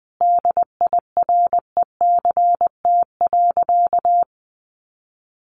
Morse code, heard as a keyed tone, about 20 wpm, 700 Hz. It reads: DIRECT.
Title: Sentences from Top 1500 Words 20wpm